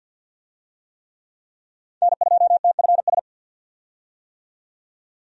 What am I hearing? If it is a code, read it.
Morse code, heard as a keyed tone, about 50 wpm, 700 Hz. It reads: N2TFS
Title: USA Call Signs - 5 Characters - 50wpm